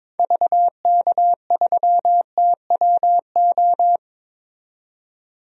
Morse code, heard as a keyed tone, about 22 wpm, 700 Hz. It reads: VK3TWO
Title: DXCC-100 Call Signs - 6 Characters - 22wpm